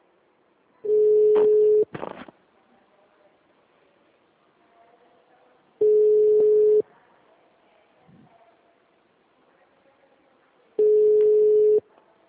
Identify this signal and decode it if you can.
Ringback tone (Europe, 425 Hz).
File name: out- -2015- - - .19553